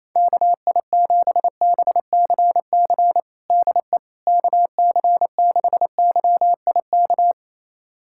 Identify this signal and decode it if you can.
Morse code, heard as a keyed tone, about 28 wpm, 700 Hz. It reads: KI7BCC DE KC6YIK
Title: USA Callsign DE Callsign 28wpm